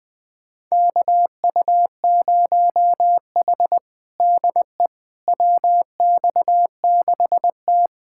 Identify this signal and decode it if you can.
Morse code, heard as a keyed tone, about 20 wpm, 700 Hz. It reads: KU0H DE WX6T
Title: USA Callsign DE Callsign - 4 Characters - 20wpm